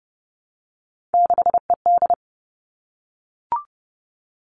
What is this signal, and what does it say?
Morse code, heard as a keyed tone, about 30 wpm, 700 Hz. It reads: 6ED
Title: All Cut Numbers - 3-Digit 30wpm